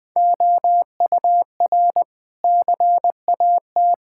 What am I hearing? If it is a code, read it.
Morse code, heard as a keyed tone, about 21 wpm, 700 Hz. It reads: OUR CAT
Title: Sentences with 5-Word Phrases - 20wpm